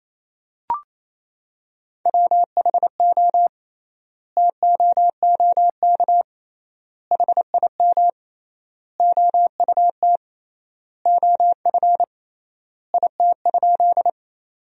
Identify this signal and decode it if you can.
Morse code, heard as a keyed tone, about 28 wpm, 700 Hz. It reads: WHO TOOK HIM OUT OF IT?
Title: 3x Word-Spacing - Sentences from Top 300 Words - 28wpm